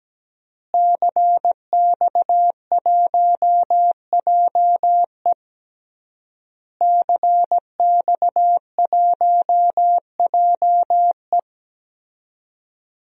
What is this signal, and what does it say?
Morse code, heard as a keyed tone, about 17 wpm, 700 Hz. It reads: CX1JE CX1JE
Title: DXCC-100 Call Signs - 5 Characters - Repeated 3x - 17wpm